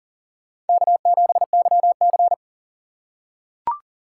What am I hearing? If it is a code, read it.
Morse code, heard as a keyed tone, about 40 wpm, 700 Hz. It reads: K7YC
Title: USA Call Signs - 3x Character Spacing - 40wpm